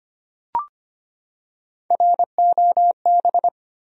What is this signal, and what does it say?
Morse code, heard as a keyed tone, about 25 wpm, 700 Hz. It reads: ROB
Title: USA Common Names - Max 5 Letters - 25wpm